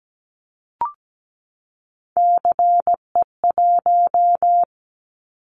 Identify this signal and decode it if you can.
Morse code, heard as a keyed tone, about 17 wpm, 700 Hz. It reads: CE1
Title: Callsign Prefixes for Top 100 DXCC - 17wpm